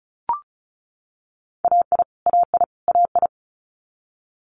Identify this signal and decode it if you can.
Morse code, heard as a keyed tone, about 36 wpm, 700 Hz. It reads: AI AI AI
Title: Multi-letter Phonemes Repeated 3x 35wpm